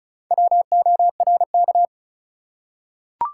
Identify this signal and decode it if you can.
Morse code, heard as a keyed tone, about 36 wpm, 700 Hz. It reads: WORK
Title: Four Letter Words - 35wpm